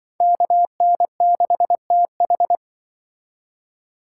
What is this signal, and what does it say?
Morse code, heard as a keyed tone, about 25 wpm, 700 Hz. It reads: KN6TH
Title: Speed Racing - USA 2x2 Call Signs 20wpm